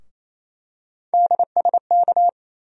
Morse code, heard as a keyed tone, about 28 wpm, 700 Hz. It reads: DSK
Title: USA Call Sign Postfixes - 3 Letters 28wpm